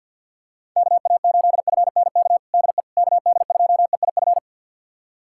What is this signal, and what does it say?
Morse code, heard as a keyed tone, about 50 wpm, 690 Hz. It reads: KA8FNK DE KD2EIF
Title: USA Callsign DE Callsign - 6 Characters - 50wpm